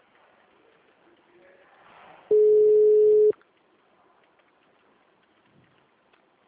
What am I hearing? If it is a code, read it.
Ringback tone (Europe, 425 Hz).